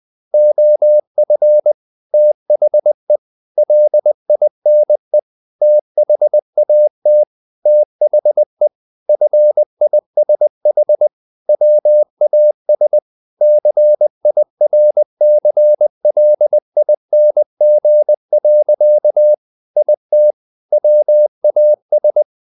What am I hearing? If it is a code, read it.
Morse code, heard as a keyed tone, about 20 wpm, 600 Hz. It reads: OF THE LINE THAT THE FISH WAS CIRCLING. IT WAS
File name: the old man and the sea part 2 20wpm